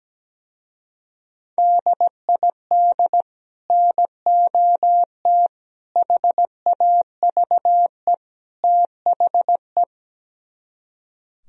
Morse code, heard as a keyed tone, about 17 wpm, 700 Hz. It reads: DID NOT HAVE THE
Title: Speed-Racing - Sets of 4 Words 17wpm